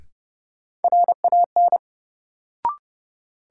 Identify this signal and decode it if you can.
Morse code, heard as a keyed tone, about 30 wpm, 700 Hz. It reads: RAN
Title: Sentences from Top 500 Words - New Word Drill - 30wpm